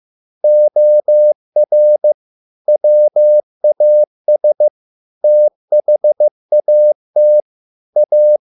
Morse code, heard as a keyed tone, about 15 wpm, 600 Hz. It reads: OR WAS THAT A